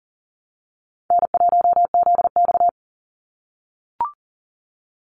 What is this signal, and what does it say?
Morse code, heard as a keyed tone, about 40 wpm, 700 Hz. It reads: N1ZX
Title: Easy Call Signs - 40wpm